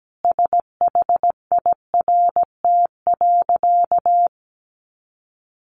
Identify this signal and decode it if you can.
Morse code, heard as a keyed tone, about 17 wpm, 700 Hz. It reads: SHIRT.
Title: Additional Easy Phrases 17wpm